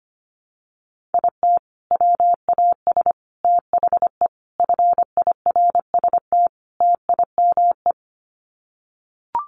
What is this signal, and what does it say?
Morse code, heard as a keyed tone, about 25 wpm, 700 Hz. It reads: IT WAS THE FIRST TIME
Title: Sets of 5 Words 25wpm